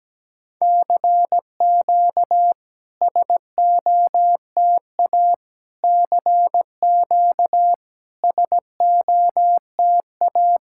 Morse code, heard as a keyed tone, about 18 wpm, 700 Hz. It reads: CQ SOTA CQ SOTA
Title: SOTA QSO Practice 17wpm